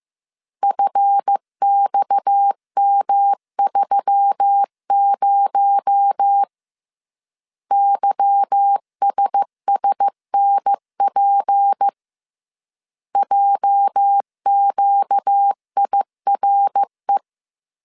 Morse code, heard as a keyed tone, about 15 wpm, 800 Hz. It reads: FXM30 YSSNP JQIRE
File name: CW / DET / 1 9 1500C S C 14wpm 3245